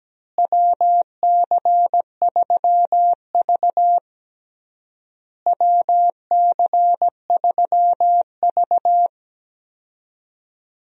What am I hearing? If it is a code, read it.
Morse code, heard as a keyed tone, about 17 wpm, 700 Hz. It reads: WC3V WC3V
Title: USA Call Signs - 4 Characters - Repeated 3x - 17wpm